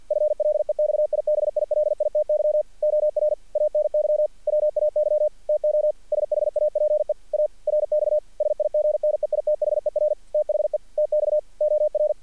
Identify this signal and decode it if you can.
Morse code, heard as a keyed tone, about 48 wpm, 600 Hz. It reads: FLEXIBILITY OR ANY WAY TO SHAPE A RX SIG NEITHER THE TX OR
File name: archived / CRSnet